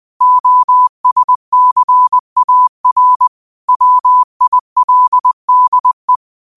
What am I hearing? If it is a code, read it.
Morse code, heard as a keyed tone, about 20 wpm, 1000 Hz. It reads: OSCAR WILDE
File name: qotdfiles / 20wpm